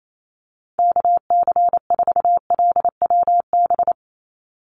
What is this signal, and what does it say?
Morse code, heard as a keyed tone, about 28 wpm, 700 Hz. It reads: KC4LWB
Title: USA Call Signs 28wpm